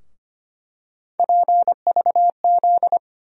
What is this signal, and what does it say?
Morse code, heard as a keyed tone, about 25 wpm, 700 Hz. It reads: PVZ
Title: USA Call Sign Postfixes - 3 Letters 25wpm